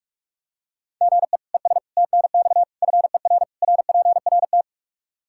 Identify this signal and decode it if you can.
Morse code, heard as a keyed tone, about 46 wpm, 700 Hz. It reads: GE ES TNX FER RPRT
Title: K1USN SST QSO Practice 45wpm